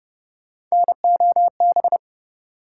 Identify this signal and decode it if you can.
Morse code, heard as a keyed tone, about 31 wpm, 700 Hz. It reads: NOB